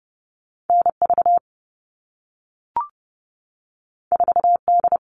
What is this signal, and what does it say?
Morse code, heard as a keyed tone, about 30 wpm, 700 Hz. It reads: NV 4D
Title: All Cut Numbers - 2-Digit 30wpm